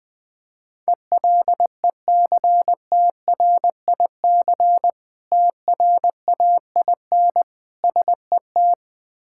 Morse code, heard as a keyed tone, about 20 wpm, 700 Hz. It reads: ELECTRIC TRAIN SET
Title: Difficult Words with Hints - 20wpm